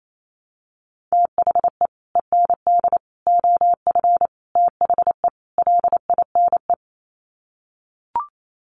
Morse code, heard as a keyed tone, about 28 wpm, 700 Hz. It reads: THE END OF THE LINE
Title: Sets of 5 Words Encore2 28wpm